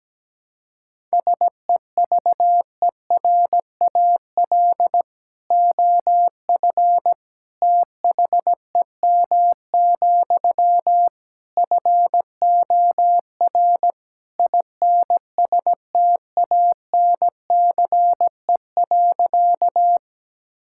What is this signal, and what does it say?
Morse code, heard as a keyed tone, about 17 wpm, 700 Hz. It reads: SEVERAL OF THEM, FOR INSTANCE.
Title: Sentences from Top 1000 Words - Review Part 5 - 17wpm